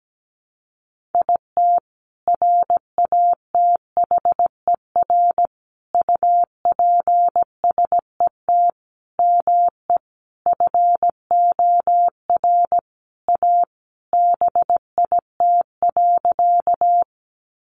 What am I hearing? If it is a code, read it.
Morse code, heard as a keyed tone, about 17 wpm, 700 Hz. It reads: IT RATHER UPSET ME FOR A BIT.
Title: Sentences from Top 900 Words 17wpm